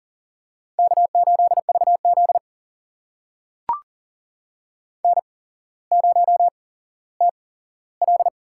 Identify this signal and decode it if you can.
Morse code, heard as a keyed tone, about 40 wpm, spaced as for 17 wpm, 700 Hz. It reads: K8VZ N0TL
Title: USA Call Signs - 4 Characters - 8x Character Spacing - 40wpm